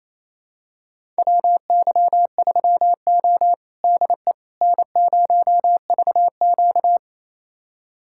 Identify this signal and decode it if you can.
Morse code, heard as a keyed tone, about 28 wpm, 700 Hz. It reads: WY3O DE N0VQ
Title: USA Callsign DE Callsign - 4 Characters - 28wpm